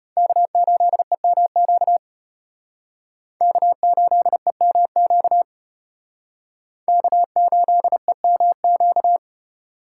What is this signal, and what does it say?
Morse code, heard as a keyed tone, about 34 wpm, 700 Hz. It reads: K8EMQ K8EMQ K8EMQ